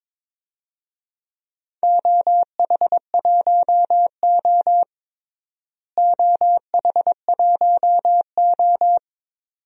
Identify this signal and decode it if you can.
Morse code, heard as a keyed tone, about 21 wpm, 700 Hz. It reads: OH1O OH1O
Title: DXCC-100 Call Signs - 4 Characters - Repeated 2x - 22wpm